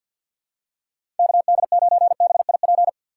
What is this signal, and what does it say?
Morse code, heard as a keyed tone, about 53 wpm, 700 Hz. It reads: KD9BIP
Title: USA Call Signs - 6 Characters - 6x Character Spacing - 50wpm